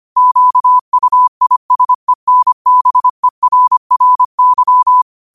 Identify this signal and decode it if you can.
Morse code, heard as a keyed tone, about 25 wpm, 1000 Hz. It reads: QUISENBERRY
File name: qotdfiles / 25wpm